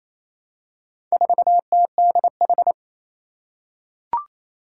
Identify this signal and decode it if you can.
Morse code, heard as a keyed tone, about 28 wpm, 700 Hz. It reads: 4TDH
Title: Four Challenging Letter-Number Combinations 28wpm